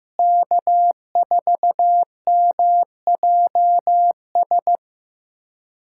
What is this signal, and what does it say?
Morse code, heard as a keyed tone, about 15 wpm, 700 Hz. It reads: K4MJS
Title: USA Callsign DE Callsign - 5 Characters - 15wpm